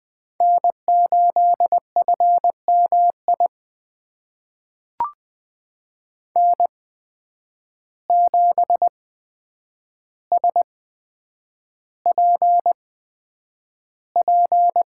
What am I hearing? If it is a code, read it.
Morse code, heard as a keyed tone, about 20 wpm, spaced as for 9 wpm, 700 Hz. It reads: N8FMI N7SPP
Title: USA Call Signs - 5 Characters - 8x Character Spacing - 20wpm